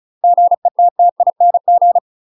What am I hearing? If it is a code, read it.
Morse code, heard as a keyed tone, about 34 wpm, 700 Hz. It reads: GETTING
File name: UK MASTER QSOs TRAINING / 35WPM / UK MASTER QSOs 0025